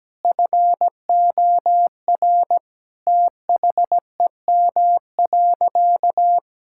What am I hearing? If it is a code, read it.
Morse code, heard as a keyed tone, about 17 wpm, 700 Hz. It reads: FOR THEM.
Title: Sentences from Top 1000 Words - Review Part 5 - 17wpm